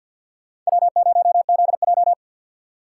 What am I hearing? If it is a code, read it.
Morse code, heard as a keyed tone, about 56 wpm, 700 Hz. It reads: W0ZJ
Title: Speed Racing - USA 1x2 Call Signs 50wpm